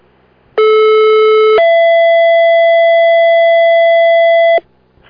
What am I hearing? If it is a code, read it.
Two-tone page, heard as a one-second tone, then a three-second tone, about 430 then 670 Hz. It is heard from 0.6 to 4.6 s.